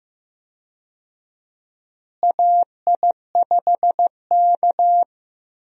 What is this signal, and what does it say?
Morse code, heard as a keyed tone, about 15 wpm, 700 Hz. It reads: AI5K
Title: USA Call Signs - 4 Characters - Repeated 2x - 15wpm